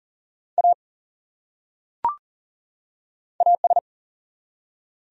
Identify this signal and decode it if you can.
Morse code, heard as a keyed tone, about 40 wpm, 700 Hz. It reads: A AS
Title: Top 100 Words 40wpm